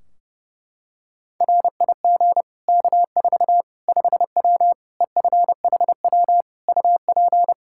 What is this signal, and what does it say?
Morse code, heard as a keyed tone, about 32 wpm, 700 Hz. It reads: RIG K4 5W EFHW UP
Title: Practice QSO - Exchange 2 - Info - 30wpm